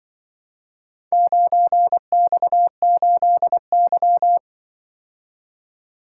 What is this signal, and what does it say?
Morse code, heard as a keyed tone, about 24 wpm, 700 Hz. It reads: 9X8Y
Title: Speed-Racing - Four Letter-Number Combinations - 20wpm